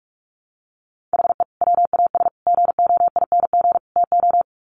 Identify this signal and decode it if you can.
Morse code, heard as a keyed tone, about 44 wpm, 700 Hz. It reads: HE WAS GOING TO
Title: Sets of 4 Words 45wpm